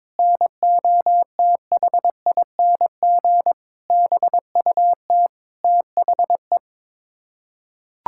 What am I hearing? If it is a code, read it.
Morse code, heard as a keyed tone, about 22 wpm, 700 Hz. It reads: NOTHING BUT THE
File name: Sets of 9 Words - 22wpm